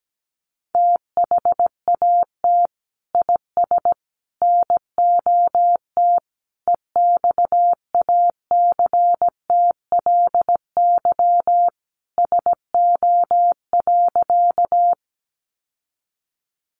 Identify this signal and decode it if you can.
Morse code, heard as a keyed tone, about 17 wpm, 700 Hz. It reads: THAT IS NOT EXACTLY SO.
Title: Sentences from Top 900 Words 17wpm